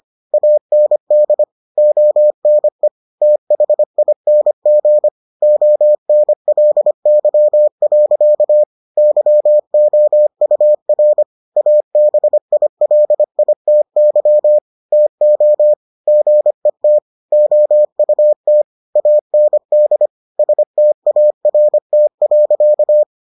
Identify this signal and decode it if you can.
Morse code, heard as a keyed tone, about 25 wpm, 600 Hz. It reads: AND ONE THING ONLY. YOUR ABILITY TO GET OUT AND START.